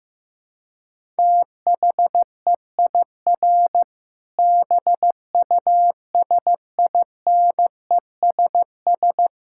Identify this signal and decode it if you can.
Morse code, heard as a keyed tone, about 15 wpm, 700 Hz. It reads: THEIR BUSINESS
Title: Sentences from Top 1000 Words - Review Part 6 - 15wpm